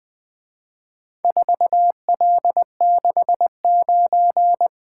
Morse code, heard as a keyed tone, about 23 wpm, 700 Hz. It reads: 4L69
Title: Speed-Racing - Four Letter-Number Combinations - 20wpm